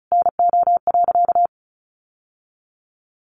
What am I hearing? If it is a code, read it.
Morse code, heard as a keyed tone, about 34 wpm, 700 Hz. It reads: NO.
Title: Sentences from Top 1200 Words 35wpm